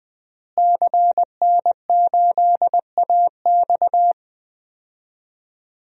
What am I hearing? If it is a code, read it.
Morse code, heard as a keyed tone, about 20 wpm, 700 Hz. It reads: CN8AX
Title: DXCC-100 Call Signs - 5 Characters - Repeated 2x - 20wpm